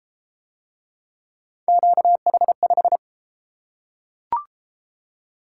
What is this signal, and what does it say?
Morse code, heard as a keyed tone, about 30 wpm, 700 Hz. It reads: QH5